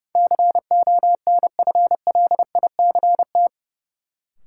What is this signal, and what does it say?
Morse code, heard as a keyed tone, about 32 wpm, 700 Hz. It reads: CONFLICT
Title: Sentences from Top 1300 Words - New Word Drill Alt - 30wpm